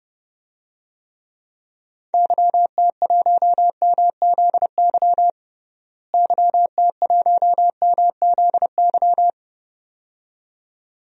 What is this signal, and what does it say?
Morse code, heard as a keyed tone, about 32 wpm, 700 Hz. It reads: YT1MZY YT1MZY
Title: DXCC-100 Call Signs - 6 Characters - Repeated 2x - 30wpm